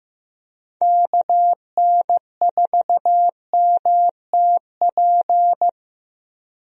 Morse code, heard as a keyed tone, about 15 wpm, 700 Hz. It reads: KN4MTP
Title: USA Call Signs - 6 Characters - Repeated 3x - 15wpm